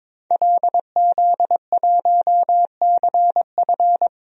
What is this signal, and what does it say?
Morse code, heard as a keyed tone, about 22 wpm, 700 Hz. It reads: LZ1CF
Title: DXCC-100 Call Signs - 5 Characters - 22wpm